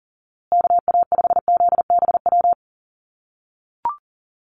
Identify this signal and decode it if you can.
Morse code, heard as a keyed tone, about 40 wpm, 700 Hz. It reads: KA5ZBW
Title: USA Call Signs Repeated 2x 40wpm